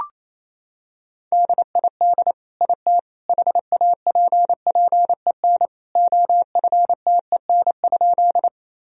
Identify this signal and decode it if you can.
Morse code, heard as a keyed tone, about 28 wpm, 700 Hz. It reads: DID IT HAPPEN OFTEN?
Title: Sentences from Top 1000 Words - Review Part 1 - 28wpm